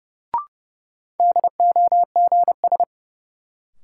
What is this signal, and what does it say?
Morse code, heard as a keyed tone, about 30 wpm, 700 Hz. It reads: DOGS
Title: Sentences from Top 1600 Words - New Word Drill Alt - 30wpm